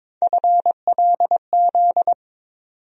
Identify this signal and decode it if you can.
Morse code, heard as a keyed tone, about 22 wpm, 700 Hz. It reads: FLZ
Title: USA Call Sign Postfixes - 3 Letters 22wpm